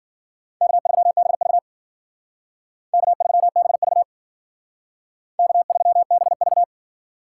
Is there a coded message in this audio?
Yes: Morse code from 0.6 to 6.6 s, heard as a keyed tone, about 51 wpm, 700 Hz. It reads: K3BV K3BV K3BV